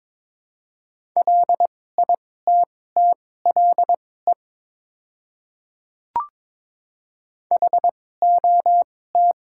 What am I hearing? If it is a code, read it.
Morse code, heard as a keyed tone, about 22 wpm, spaced as for 16 wpm, 700 Hz. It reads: LITTLE HOT
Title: 2x Character Spacing - Top 300 Words - 22wpm